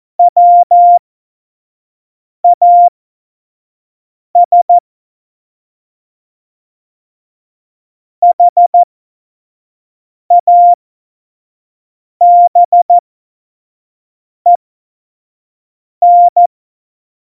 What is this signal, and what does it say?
Morse code, heard as a keyed tone, about 14 wpm, spaced as for 5 wpm, 700 Hz. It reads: WAS HABEN